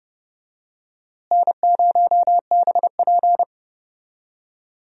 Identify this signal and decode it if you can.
Morse code, heard as a keyed tone, about 31 wpm, 700 Hz. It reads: N0BP
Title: USA Call Signs - 4 Characters - Repeated 3x - 30wpm